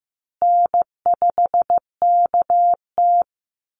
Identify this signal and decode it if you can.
Morse code, heard as a keyed tone, about 15 wpm, 700 Hz. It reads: N5KT
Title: Easy Call Signs Repeated 3x - 15wpm